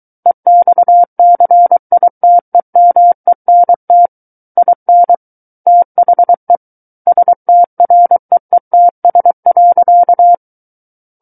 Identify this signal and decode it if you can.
Morse code, heard as a keyed tone, about 23 wpm, 700 Hz. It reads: EXCITEMENT IN THE STREETS.
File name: Book One / 14-Chapter 14 - 23 WPM